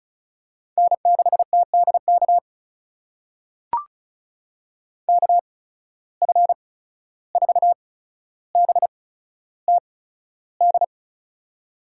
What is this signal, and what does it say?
Morse code, heard as a keyed tone, about 36 wpm, spaced as for 14 wpm, 700 Hz. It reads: N6TDK KF4BTD
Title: USA Call Signs - 8x Character Spacing - 35wpm